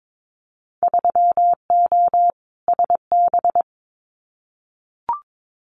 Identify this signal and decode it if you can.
Morse code, heard as a keyed tone, about 22 wpm, 700 Hz. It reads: 3O SB
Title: Winter Field Day Exchanges Encore 22wpm